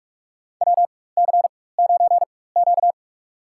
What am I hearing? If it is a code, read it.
Morse code, heard as a keyed tone, about 41 wpm, spaced as for 20 wpm, 700 Hz. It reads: WC9Q
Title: USA Call Signs - 4 Characters - 4x Character Spacing - 45wpm